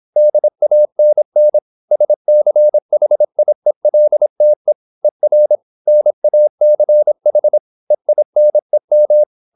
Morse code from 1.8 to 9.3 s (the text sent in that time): SCHIELTE ER NACH EINEM